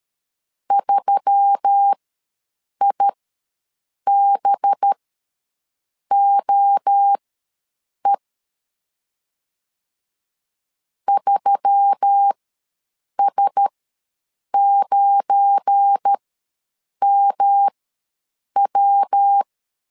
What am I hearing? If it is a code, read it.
Morse code, heard as a keyed tone, about 13 wpm, spaced as for 7 wpm, 800 Hz. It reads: 3IBOE 3S9MW
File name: LEZIONI / 7 GF9Z / 7 GF9 ws 12wpm 3344